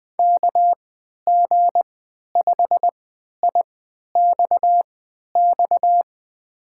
Morse code, heard as a keyed tone, about 20 wpm, spaced as for 11 wpm, 700 Hz. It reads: KG5IXX
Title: USA Call Signs - 6 Characters - 3x Character Spacing - 20wpm